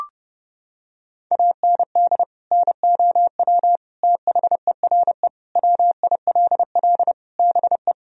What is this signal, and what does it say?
Morse code, heard as a keyed tone, about 30 wpm, 700 Hz. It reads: AND NOW THERE WILL BE
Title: Sentences from Top 1000 Words - Review Part 2 - 30wpm